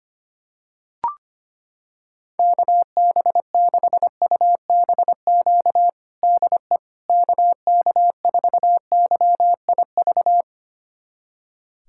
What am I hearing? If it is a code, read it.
Morse code, heard as a keyed tone, about 25 wpm, 700 Hz. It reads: KB6UBQ DE KK4YIV
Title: USA Callsign DE Callsign - 6 Characters - 25wpm